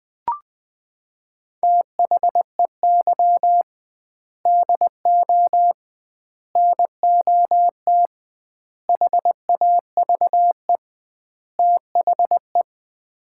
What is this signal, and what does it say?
Morse code, heard as a keyed tone, about 20 wpm, 700 Hz. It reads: THEY DO NOT HAVE THE
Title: 2x Word-Spacing - Sets of 5 Words - 20wpm